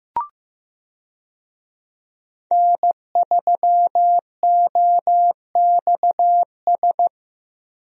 Morse code, heard as a keyed tone, about 15 wpm, 700 Hz. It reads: N3OXS
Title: USA Call Signs - 5 Characters - Repeated 2x - 15wpm